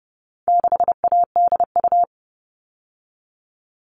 Morse code, heard as a keyed tone, about 30 wpm, 700 Hz. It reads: 6ADU
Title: All Cut Numbers - 4-Digit 30wpm